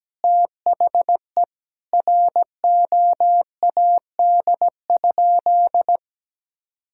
Morse code, heard as a keyed tone, about 17 wpm, 700 Hz. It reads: THE ROAD?
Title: Sentences from Top 1000 Words - Review Part 5 - 17wpm